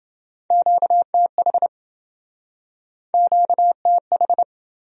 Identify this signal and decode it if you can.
Morse code, heard as a keyed tone, about 30 wpm, 700 Hz. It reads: QTH QTH